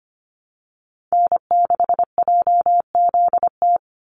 Morse code, heard as a keyed tone, about 25 wpm, 700 Hz. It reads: N6JZT
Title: NA SOTA Activator Callsigns 25wpm